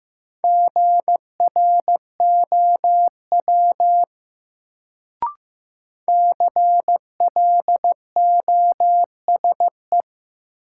Morse code, heard as a keyed tone, about 15 wpm, 700 Hz. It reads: GROW CLOSE
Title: Sentences from Top 300 Words - New Word Drill - 15wpm